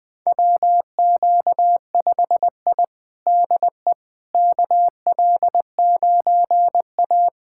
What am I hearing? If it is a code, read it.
Morse code, heard as a keyed tone, about 20 wpm, 700 Hz. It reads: WQ5I DE KL9A
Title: USA Callsign DE Callsign - 4 Characters - 20wpm